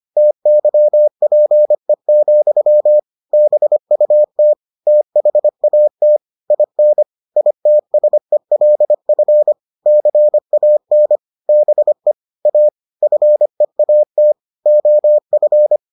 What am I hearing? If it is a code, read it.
Morse code, heard as a keyed tone, about 25 wpm, 600 Hz. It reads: TYPE, BUT THAT IN ITSELF CAN BE A FEAT OF